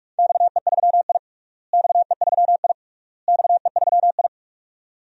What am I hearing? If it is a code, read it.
Morse code, heard as a keyed tone, about 42 wpm, 700 Hz. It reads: XE3I XE3I XE3I
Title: DXCC-100 Call Signs - 4 Characters - Repeated 3x - 45wpm